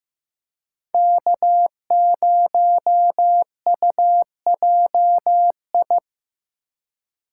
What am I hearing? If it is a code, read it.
Morse code, heard as a keyed tone, about 15 wpm, 700 Hz. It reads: K0UJI
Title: USA Call Signs - 5 Characters - 8x Character Spacing - 15wpm